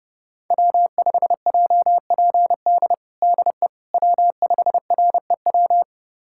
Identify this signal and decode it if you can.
Morse code, heard as a keyed tone, about 30 wpm, 700 Hz. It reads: W5JPD DE W5REW
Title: USA Callsign DE Callsign - 5 Characters - 30wpm